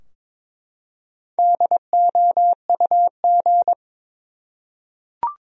Morse code, heard as a keyed tone, about 21 wpm, 700 Hz. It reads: DOUG
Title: USA Common Names - Max 5 Letters - 22wpm